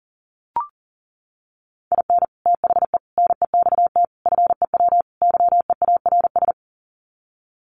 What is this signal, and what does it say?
Morse code, heard as a keyed tone, about 40 wpm, 700 Hz. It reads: IN THE NEXT FEW YEARS
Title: Sets of 5 Words Encore 40wpm